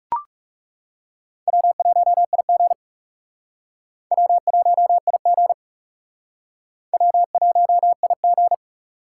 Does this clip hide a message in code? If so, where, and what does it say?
Yes: Morse code from 1.5 to 8.5 s, heard as a keyed tone, about 40 wpm, 700 Hz. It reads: W1IG W1IG W1IG